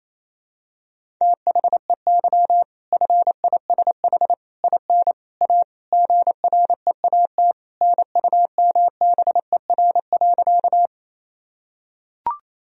Morse code, heard as a keyed tone, about 28 wpm, 700 Hz. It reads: THEY FISH IN A GREAT NUMBER.
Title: Sentences from Top 1000 Words - Review Part 1 - 28wpm